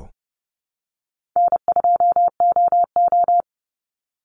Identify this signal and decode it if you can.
Morse code, heard as a keyed tone, about 30 wpm, 700 Hz. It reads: N2OO
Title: Easy Call Signs - 30wpm